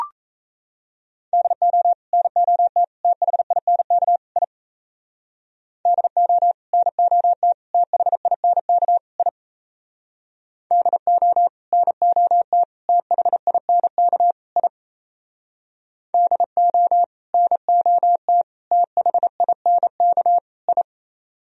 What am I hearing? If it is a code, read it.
Morse code, heard as a keyed tone, about 35 wpm, 700 Hz. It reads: DO NOT THINK I DO NOT THINK I DO NOT THINK I DO NOT THINK I